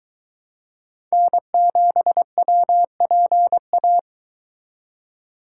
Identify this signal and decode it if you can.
Morse code, heard as a keyed tone, about 22 wpm, 700 Hz. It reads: N7WPA